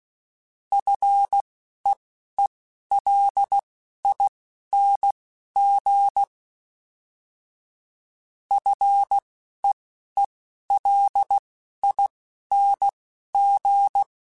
Morse code, heard as a keyed tone, about 16 wpm, spaced as for 11 wpm, 800 Hz. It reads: FEELING FEELING
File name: Short Runs / Short Runs-mp3s / SinglePrefixes @10